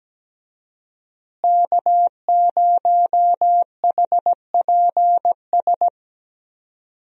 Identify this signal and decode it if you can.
Morse code, heard as a keyed tone, about 17 wpm, 700 Hz. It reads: K0HPS
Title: USA Call Signs - 5 Characters - Repeated 3x - 17wpm